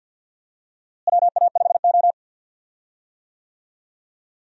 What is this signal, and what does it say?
Morse code, heard as a keyed tone, about 49 wpm, 700 Hz. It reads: WA5O
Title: USA Call Signs - 4 Characters - Repeated 2x - 50wpm